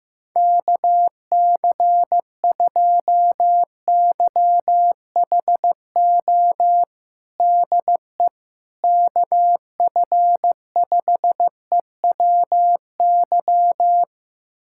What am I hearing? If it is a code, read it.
Morse code, heard as a keyed tone, about 16 wpm, 700 Hz. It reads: KC2YHO DE KF5EWY
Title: USA Callsign DE Callsign - 6 Characters - 15wpm